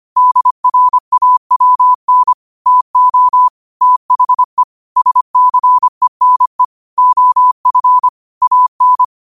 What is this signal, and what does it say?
Morse code, heard as a keyed tone, about 25 wpm, 1000 Hz. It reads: DRAWN TO THE SCENE OF AN
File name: 25wpm